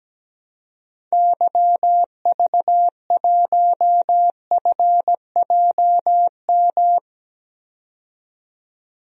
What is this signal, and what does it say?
Morse code, heard as a keyed tone, about 17 wpm, 700 Hz. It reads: YV1FJM
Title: DXCC-100 Call Signs - 6 Characters - 17wpm